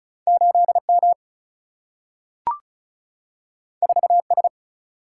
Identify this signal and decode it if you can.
Morse code, heard as a keyed tone, about 36 wpm, 700 Hz. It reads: 8 M 4 S
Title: Single-Letter Single-Number 35wpm